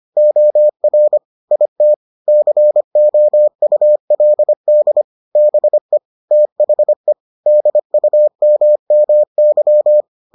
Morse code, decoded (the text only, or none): OR IT COULD BE THE DUMMY